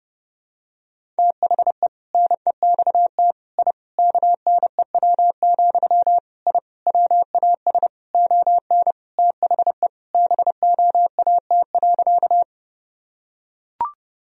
Morse code, heard as a keyed tone, about 30 wpm, 700 Hz. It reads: THE NEXT I KNEW, I WAS ON THE BOAT.
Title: Sentences from Top 1000 Words - Review Part 3 - 30wpm